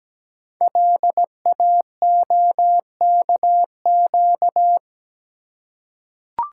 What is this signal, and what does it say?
Morse code, heard as a keyed tone, about 17 wpm, 700 Hz. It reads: LAOKQ
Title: Code Groups - 5 Letters - 17wpm